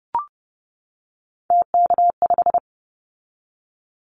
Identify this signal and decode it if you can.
Morse code, heard as a keyed tone, about 30 wpm, 700 Hz. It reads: TK5
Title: Callsign Prefixes for Top 100 DXCC - 30wpm